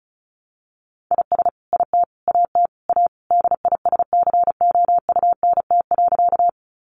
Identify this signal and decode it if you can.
Morse code, heard as a keyed tone, about 34 wpm, 700 Hz. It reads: IS IT AT A DISCOUNT.
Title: Sentences from Top 1200 Words 35wpm